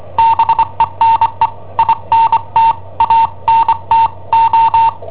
Morse code, heard as a keyed tone, about 25 wpm, spaced as for 22 wpm, 910 Hz. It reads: BENE INTANTO
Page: Cassetta esercizi preparata da I7OHP per IW5ELP